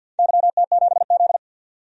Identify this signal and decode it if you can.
Morse code, heard as a keyed tone, about 52 wpm, 700 Hz. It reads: YT7Z
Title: DXCC-100 Call Signs - 4 Characters - 50wpm